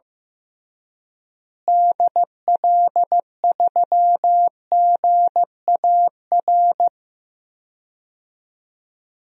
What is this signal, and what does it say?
Morse code, heard as a keyed tone, about 15 wpm, 700 Hz. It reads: DL3GAR
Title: DXCC-100 Call Signs - 6 Characters - Repeated 3x - 15wpm